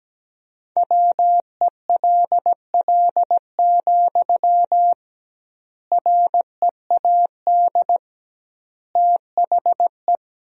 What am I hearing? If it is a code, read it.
Morse code, heard as a keyed tone, about 17 wpm, 700 Hz. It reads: WELL, READ THE
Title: 2x Word-Spacing - Sentences from Top 300 Words - 17wpm